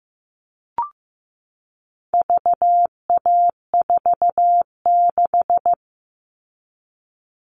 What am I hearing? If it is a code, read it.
Morse code, heard as a keyed tone, about 15 wpm, 700 Hz. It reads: VA46
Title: All Cut Numbers - 4-Digit 15wpm